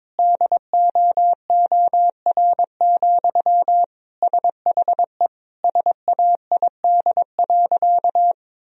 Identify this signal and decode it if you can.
Morse code, heard as a keyed tone, about 22 wpm, 700 Hz. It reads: DOOR, SHE SAID.
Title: Sentences from Top 1000 Words - Review Part 2 - 22wpm